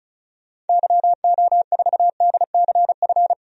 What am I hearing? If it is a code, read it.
Morse code, heard as a keyed tone, about 36 wpm, 700 Hz. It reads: YO4DCF
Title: DXCC-100 Call Signs - 6 Characters - Repeated 2x - 35wpm